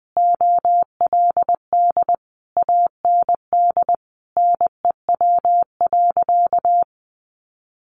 Morse code, heard as a keyed tone, about 20 wpm, 700 Hz. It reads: OLD AND NEW.
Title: Sentences from Top 1500 Words 20wpm